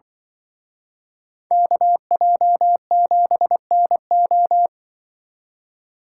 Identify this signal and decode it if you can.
Morse code, heard as a keyed tone, about 24 wpm, 700 Hz. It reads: KJ7NO
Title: Speed Racing - USA 2x2 Call Signs 20wpm